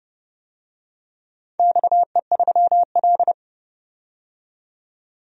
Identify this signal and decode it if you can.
Morse code, heard as a keyed tone, about 30 wpm, 700 Hz. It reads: XE3L
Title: DXCC-100 Call Signs - 4 Characters - 30wpm